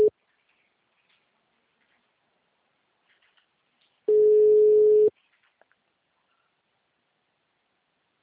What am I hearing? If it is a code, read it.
Ringback tone (Europe, 425 Hz).
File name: out- -2041- - - .8923